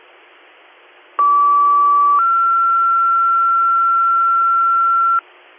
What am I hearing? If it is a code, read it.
Two-tone page, heard as a one-second tone, then a three-second tone, about 1150 then 1360 Hz, with noise. It is heard from 1.2 to 5.2 s.